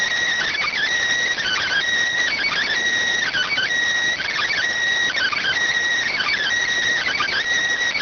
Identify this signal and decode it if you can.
Emergency Alert System (SAME) tones (header bursts), heard as harsh buzzing bursts. It is heard from 0.7 to 6.9 s.